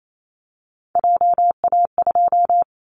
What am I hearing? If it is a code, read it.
Morse code, heard as a keyed tone, about 28 wpm, 700 Hz. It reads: JA2
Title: Callsign Prefixes for Top 50 DXCC - 28wpm